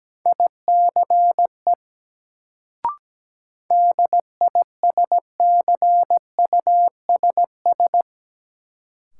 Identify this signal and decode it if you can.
Morse code, heard as a keyed tone, about 17 wpm, 700 Hz. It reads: ICE DISCUSS
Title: Sentences from Top 1000 Words - New Word Drill - 17wpm